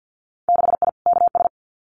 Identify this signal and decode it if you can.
Morse code, heard as a keyed tone, about 50 wpm, 700 Hz. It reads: 6I KS
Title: Winter Field Day Exchanges 50wpm